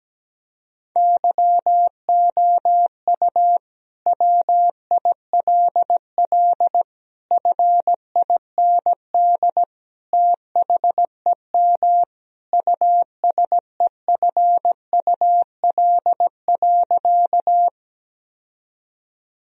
Morse code, heard as a keyed tone, about 17 wpm, 700 Hz. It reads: YOU WILL FIND THEM USEFUL.
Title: Sentences from Top 1000 Words - Review Part 4 - 17wpm